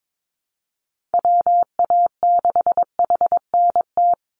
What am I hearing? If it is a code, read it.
Morse code, heard as a keyed tone, about 22 wpm, 700 Hz. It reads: WA6HNT
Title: USA Call Signs 22wpm